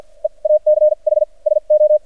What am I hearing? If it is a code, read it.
Morse code, heard as a keyed tone, about 53 wpm, spaced as for 38 wpm, 610 Hz. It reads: EACH SO
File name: archived / CRSnet